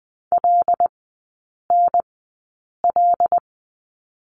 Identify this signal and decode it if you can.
Morse code, heard as a keyed tone, about 20 wpm, spaced as for 8 wpm, 700 Hz. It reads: LNL
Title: 074 Lesson 13e - Letter L - Three Characters 20wpm